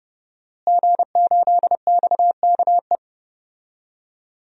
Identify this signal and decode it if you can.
Morse code, heard as a keyed tone, about 30 wpm, 700 Hz. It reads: G8XKE
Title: DXCC-100 Call Signs - 5 Characters - Repeated 2x - 30wpm